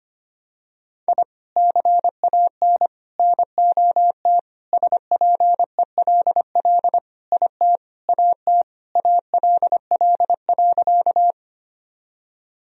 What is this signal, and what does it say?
Morse code, heard as a keyed tone, about 25 wpm, 700 Hz. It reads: I CAN NOT SPELL IT AT ALL.
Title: Sentences from Top 1000 Words - Review Part 1 - 25wpm